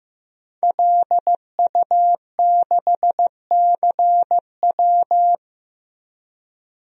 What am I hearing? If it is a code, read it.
Morse code, heard as a keyed tone, about 15 wpm, 700 Hz. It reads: LU6CW
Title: DXCC-100 Call Signs - 5 Characters - Repeated 2x - 15wpm